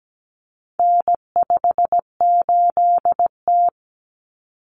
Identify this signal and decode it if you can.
Morse code, heard as a keyed tone, about 17 wpm, 700 Hz. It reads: N58T
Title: Common Cut Numbers - 4-Digit 17wpm